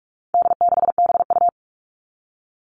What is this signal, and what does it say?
Morse code, heard as a keyed tone, about 44 wpm, 700 Hz. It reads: D6BU
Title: All Cut Numbers - 4-Digit 45wpm